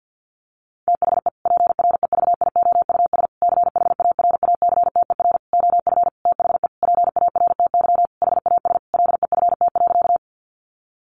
Morse code, heard as a keyed tone, about 50 wpm, 700 Hz. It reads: THE PREVIOUS CHARACTER OF THE PARTY HAS LEFT.
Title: Sentences from Top 1500 Words 50wpm